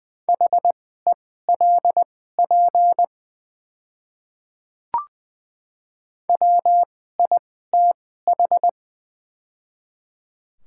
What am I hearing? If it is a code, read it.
Morse code, heard as a keyed tone, about 20 wpm, spaced as for 14 wpm, 690 Hz. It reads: HELP WITH
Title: 2x Character Spacing - Top 500 Words - 20wpm